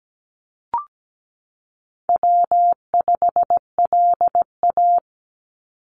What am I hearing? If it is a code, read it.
Morse code, heard as a keyed tone, about 17 wpm, 700 Hz. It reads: W5LA
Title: Call Signs Repeated 2x - 17wpm